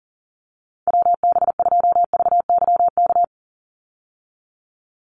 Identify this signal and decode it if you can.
Morse code, heard as a keyed tone, about 40 wpm, 700 Hz. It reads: WB2VYK
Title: Call Signs Repeated 2x - 40wpm